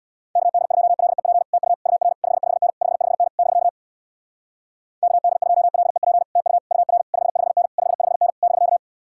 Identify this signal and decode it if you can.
Morse code, heard as a keyed tone, about 64 wpm, 700 Hz. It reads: KD2LEP TU UR 55N 55N <BK> KD2LEP TU UR 55N 55N <BK>